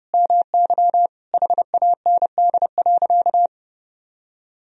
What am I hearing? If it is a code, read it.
Morse code, heard as a keyed tone, about 32 wpm, 700 Hz. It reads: MY HAND.
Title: Sentences from Top 1000 Words - Review Part 1 - 30wpm